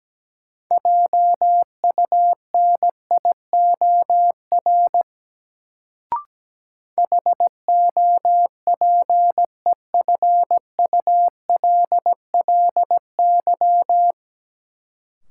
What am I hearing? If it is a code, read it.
Morse code, heard as a keyed tone, about 17 wpm, 700 Hz. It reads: JUNIOR HOPEFULLY
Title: Sentences from Top 1500 Words - New Word Drill Alt - 17wpm